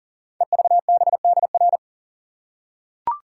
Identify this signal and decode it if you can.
Morse code, heard as a keyed tone, about 42 wpm, 700 Hz. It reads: EVBDR
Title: Code Groups - 5 Letter:Numbers - 40wpm